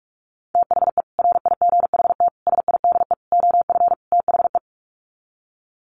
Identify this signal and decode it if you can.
Morse code, heard as a keyed tone, about 44 wpm, 700 Hz. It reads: THE RIGHT SIDE OF THE
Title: Sets of 5 Words Encore 45wpm